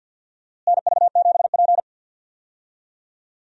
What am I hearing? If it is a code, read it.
Morse code, heard as a keyed tone, about 52 wpm, 700 Hz. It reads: NV7P
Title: USA Call Signs - 4 Characters - Repeated 3x - 50wpm